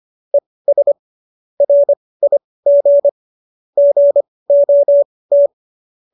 Morse code, heard as a keyed tone, about 25 wpm, spaced as for 18 wpm, 570 Hz. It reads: ES RIG GOT